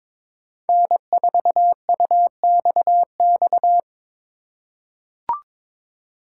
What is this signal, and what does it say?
Morse code, heard as a keyed tone, about 22 wpm, 700 Hz. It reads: N4UXX
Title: USA Call Signs - 5 Characters - 4x Character Spacing - 22wpm